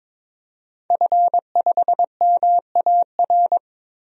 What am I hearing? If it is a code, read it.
Morse code, heard as a keyed tone, about 22 wpm, 700 Hz. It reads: F5MAR
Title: DXCC-100 Call Signs - 5 Characters - Repeated 2x - 22wpm